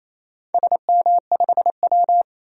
Morse code, heard as a keyed tone, about 28 wpm, 700 Hz. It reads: SM5W
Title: DXCC-100 Call Signs - 4 Characters - 28wpm